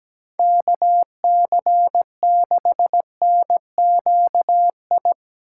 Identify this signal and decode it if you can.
Morse code, heard as a keyed tone, about 20 wpm, 700 Hz. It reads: KC6NQI
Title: Speed Racing - USA 2x3 Call Signs 17wpm